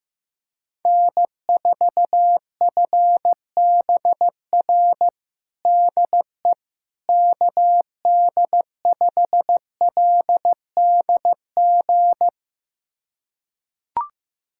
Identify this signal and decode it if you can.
Morse code, heard as a keyed tone, about 15 wpm, 700 Hz. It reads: N4FBR DE KD5LDG
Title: USA Callsign DE Callsign 15wpm